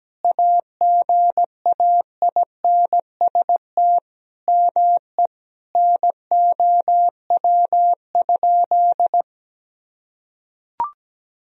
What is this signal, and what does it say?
Morse code, heard as a keyed tone, about 17 wpm, 700 Hz. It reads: AGAINST ME NOW?
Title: Sentences from Top 400 Words and Abbreviations 17wpm